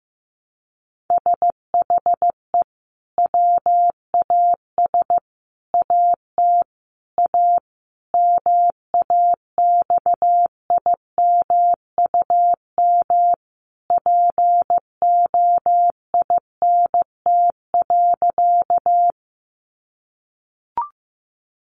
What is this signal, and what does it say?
Morse code, heard as a keyed tone, about 15 wpm, 700 Hz. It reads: SHE WAS AT A MAXIMUM POINT.
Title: Sentences from Top 1300 Words 15wpm